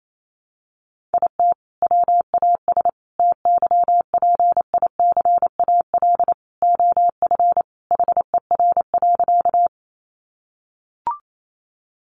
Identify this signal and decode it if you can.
Morse code, heard as a keyed tone, about 28 wpm, 700 Hz. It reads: IT WAS TYPICAL OF HER.
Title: Sentences from Top 1100 Words 28wpm